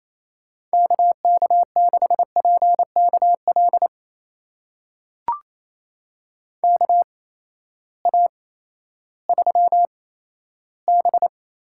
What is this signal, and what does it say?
Morse code, heard as a keyed tone, about 28 wpm, spaced as for 14 wpm, 700 Hz. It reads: KK6PKL KA3B
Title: USA Call Signs - 6 Characters - 8x Character Spacing - 28wpm